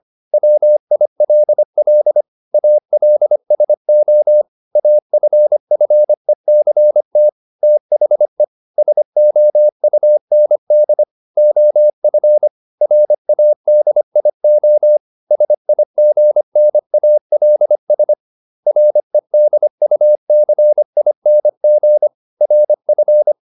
Morse code, heard as a keyed tone, about 25 wpm, 600 Hz. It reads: WILL ALSO AFFECT THE SOUND OF RADIO SIGNALS REDUCING RF